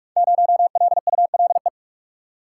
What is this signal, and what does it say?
Morse code, heard as a keyed tone, about 44 wpm, 700 Hz. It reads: 0LULE
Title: Code Groups - 5 Letter:Numbers - 45wpm